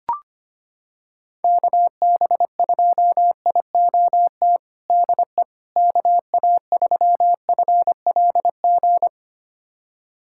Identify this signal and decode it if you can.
Morse code, heard as a keyed tone, about 25 wpm, 700 Hz. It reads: KB2IOT DE KA3FLG
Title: USA Callsign DE Callsign - 6 Characters - 25wpm